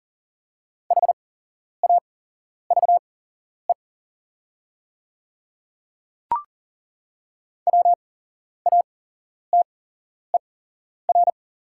Morse code, heard as a keyed tone, about 40 wpm, spaced as for 11 wpm, 700 Hz. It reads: HAVE WATER
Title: 8x Character Spacing - Top 200 Words - 40wpm